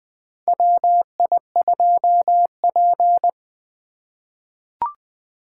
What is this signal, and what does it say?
Morse code, heard as a keyed tone, about 20 wpm, 700 Hz. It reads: WI2P
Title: USA Call Signs - 4 Characters - 2x Character Spacing - 20wpm